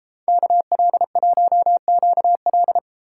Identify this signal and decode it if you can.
Morse code, heard as a keyed tone, about 33 wpm, 700 Hz. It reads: KL1QL